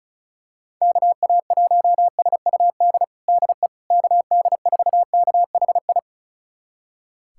Morse code, heard as a keyed tone, about 36 wpm, 700 Hz. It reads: KA1SUD DE KD4KHI
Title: USA Callsign DE Callsign - 6 Characters - 35wpm